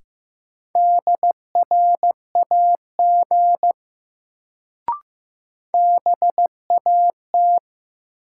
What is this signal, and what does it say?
Morse code, heard as a keyed tone, about 15 wpm, 700 Hz. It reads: DRAG BAT
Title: Sentences from Top 1500 Words - New Word Drill - 15wpm